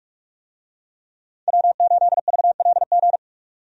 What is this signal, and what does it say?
Morse code, heard as a keyed tone, about 46 wpm, 700 Hz. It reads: W8VLG
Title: USA Call Signs - 5 Characters - 45wpm